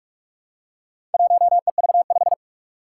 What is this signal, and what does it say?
Morse code, heard as a keyed tone, about 46 wpm, 700 Hz. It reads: 1EV5
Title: Four Challenging Letter-Number Combinations 45wpm